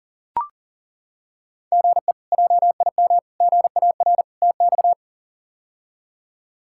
Morse code, heard as a keyed tone, about 37 wpm, 700 Hz. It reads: GE JIM GAR TX
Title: K1USN SST QSO Practice - 40wpm